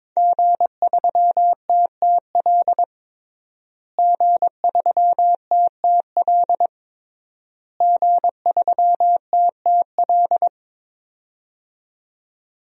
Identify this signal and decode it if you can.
Morse code, heard as a keyed tone, about 22 wpm, 700 Hz. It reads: G3TTL G3TTL G3TTL
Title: DXCC-100 Call Signs - 5 Characters - Repeated 3x - 22wpm